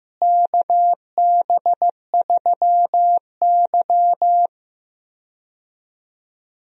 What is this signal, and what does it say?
Morse code, heard as a keyed tone, about 15 wpm, 700 Hz. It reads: KB3Y
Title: USA Call Signs - 4 Characters - Repeated 3x - 15wpm